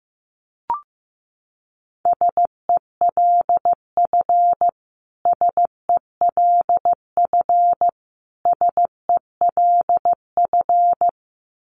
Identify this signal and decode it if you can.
Morse code, heard as a keyed tone, about 15 wpm, 700 Hz. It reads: SELF SELF SELF
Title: Top 500 Words Repeated 15wpm